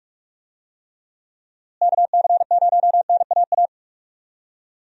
Morse code, heard as a keyed tone, about 46 wpm, 700 Hz. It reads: KC0NAA
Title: USA Call Signs - 6 Characters - Repeated 3x - 45wpm